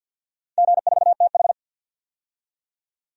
Morse code, heard as a keyed tone, about 51 wpm, 700 Hz. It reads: G4TH
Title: DXCC-100 Call Signs - 4 Characters - 50wpm